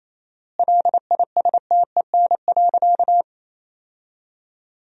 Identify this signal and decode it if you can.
Morse code, heard as a keyed tone, about 28 wpm, 700 Hz. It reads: LISTEN.
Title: 4x Word-Spacing - Sentences from Top 500 Words - 28wpm